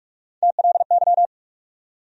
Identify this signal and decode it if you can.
Morse code, heard as a keyed tone, about 46 wpm, 700 Hz. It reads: TLY
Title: USA Call Sign Postfixes - 3 Letters 45wpm